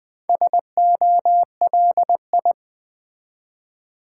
Morse code, heard as a keyed tone, about 20 wpm, 700 Hz. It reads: SOLI
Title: Build A Word - 1 Letter At A Time - 5-Letter Words - 20wpm